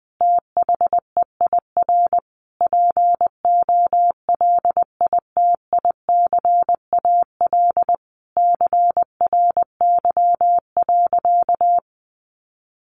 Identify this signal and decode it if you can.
Morse code, heard as a keyed tone, about 20 wpm, 700 Hz. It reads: THEIR POLITICAL CRY.
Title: Sentences from Top 1400 Words 20wpm